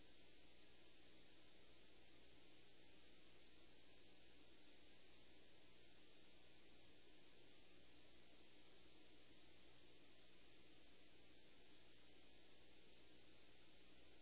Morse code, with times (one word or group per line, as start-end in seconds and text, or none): none